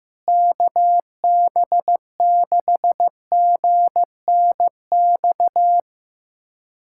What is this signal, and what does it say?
Morse code, heard as a keyed tone, about 16 wpm, 700 Hz. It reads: KB6GNX
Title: USA Callsign DE Callsign 15wpm